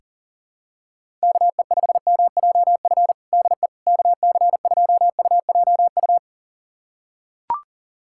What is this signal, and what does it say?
Morse code, heard as a keyed tone, about 40 wpm, 700 Hz. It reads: KE5MJF DE KC2UJU
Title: USA Callsign DE Callsign - 6 Characters - 40wpm